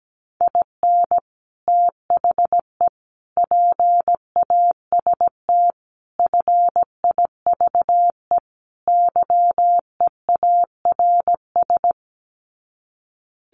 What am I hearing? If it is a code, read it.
Morse code, heard as a keyed tone, about 17 wpm, 700 Hz. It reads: IN THE PAST FIVE YEARS
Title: Sets of 5 Words Encore 17wpm